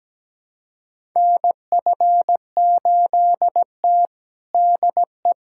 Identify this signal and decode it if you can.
Morse code, heard as a keyed tone, about 17 wpm, 700 Hz. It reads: NF8T DE
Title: USA Callsign DE Callsign - 4 Characters - 17wpm